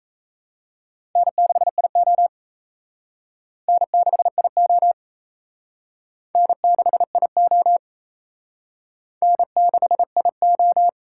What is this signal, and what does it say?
Morse code, heard as a keyed tone, about 34 wpm, 700 Hz. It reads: N6IO N6IO N6IO N6IO
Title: Speed Racing - USA Call Signs 28wpm